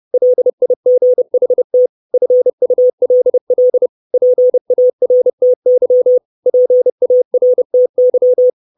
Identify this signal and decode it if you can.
Morse code, heard as a keyed tone, about 30 wpm, 500 Hz. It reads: LIGHT FULL PARTY PARTY
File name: 632 Words 30WPM